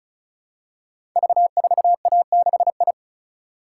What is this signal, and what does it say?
Morse code, heard as a keyed tone, about 36 wpm, 700 Hz. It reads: V4A6I
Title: Code Groups - 5 Letter:Numbers - 35wpm